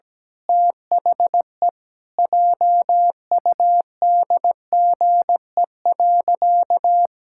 Morse code, heard as a keyed tone, about 17 wpm, 700 Hz. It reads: THE JUDGE.
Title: Pangram Challenge v2 - 17wpm